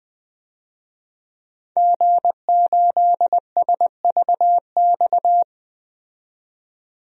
Morse code, heard as a keyed tone, about 20 wpm, 700 Hz. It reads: G8SVX
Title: DXCC-100 Call Signs - 5 Characters - 20wpm